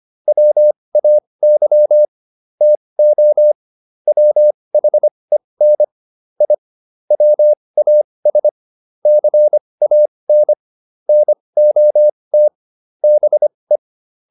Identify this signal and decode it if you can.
Morse code, heard as a keyed tone, about 25 wpm, spaced as for 20 wpm, 600 Hz. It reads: WAY TO WHEN I WAS CAN NOT BE